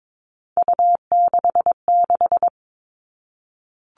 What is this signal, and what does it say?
Morse code, heard as a keyed tone, about 22 wpm, 700 Hz. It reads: U66
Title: All Cut Numbers - 3-Digit 22wpm